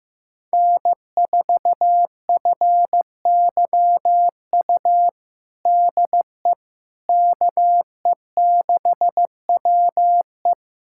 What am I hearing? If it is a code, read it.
Morse code, heard as a keyed tone, about 15 wpm, 700 Hz. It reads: N4FYU DE KE6WE
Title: USA Callsign DE Callsign - 5 Characters - 15wpm